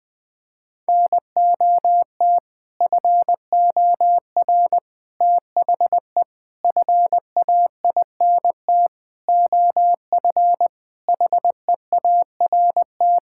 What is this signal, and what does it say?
Morse code, heard as a keyed tone, about 21 wpm, 700 Hz. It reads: NOT FOR THE FAINT OF HEART
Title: Sets of 6 Words - 20wpm